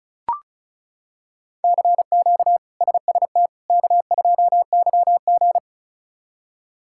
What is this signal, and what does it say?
Morse code, heard as a keyed tone, about 36 wpm, 700 Hz. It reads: CQ SST K2YG
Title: K1USN SST QSO Practice - 35wpm